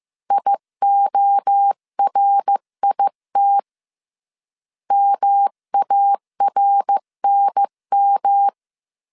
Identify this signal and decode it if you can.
Morse code, heard as a keyed tone, about 15 wpm, 800 Hz. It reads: IORIT MARNM
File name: CW / DET / 1 3 1500C S A 14wpm 2447